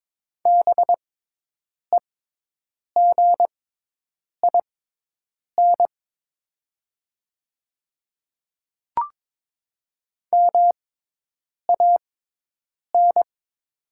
Morse code, heard as a keyed tone, about 22 wpm, spaced as for 8 wpm, 700 Hz. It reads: BEGIN MAN
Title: 6x Character Spacing - Top 300 Words - 22wpm